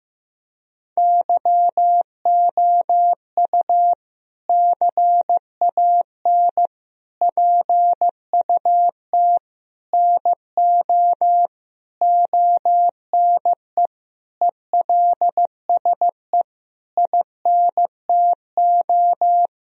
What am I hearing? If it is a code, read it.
Morse code, heard as a keyed tone, about 15 wpm, 700 Hz. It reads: YOU CAN PUT NO ONE ELSE INTO
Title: Sentences from Top 1000 Words - Review Part 2 - 15wpm